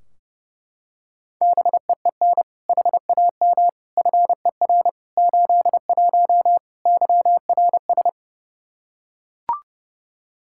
Morse code, heard as a keyed tone, about 30 wpm, 700 Hz. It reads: BEEN HAM FER 81 YRS
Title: Practice QSO - Exchange 3 - More Info - 30wpm